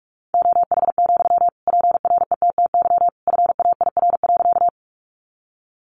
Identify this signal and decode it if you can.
Morse code, heard as a keyed tone, about 44 wpm, 700 Hz. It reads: OH, PRETTY FAIR.
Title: Sentences from Top 1300 Words 45wpm